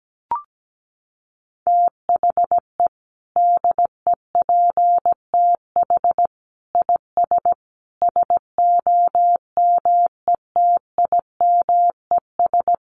Morse code, heard as a keyed tone, about 17 wpm, 700 Hz. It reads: THE DEPTH IS SOMETIMES
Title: Sentences from Top 1200 Words 17wpm